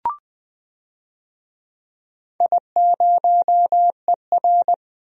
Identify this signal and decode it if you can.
Morse code, heard as a keyed tone, about 20 wpm, 700 Hz. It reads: I0ER
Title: DXCC-100 Call Signs - 4 Characters - 20wpm